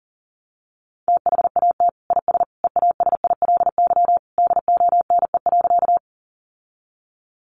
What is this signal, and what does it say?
Morse code, heard as a keyed tone, about 40 wpm, 700 Hz. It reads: THAT IS EASILY DONE.
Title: Sentences from Top 900 Words 40wpm